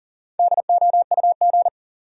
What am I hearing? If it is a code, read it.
Morse code, heard as a keyed tone, about 40 wpm, 700 Hz. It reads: DOUG
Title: USA Common Names - Max 4 Letters - 40wpm